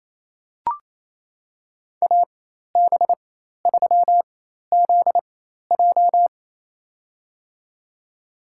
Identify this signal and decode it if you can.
Morse code, heard as a keyed tone, about 28 wpm, spaced as for 13 wpm, 700 Hz. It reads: AB3ZJ
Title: USA Call Signs - 5 Characters - 4x Character Spacing - 28wpm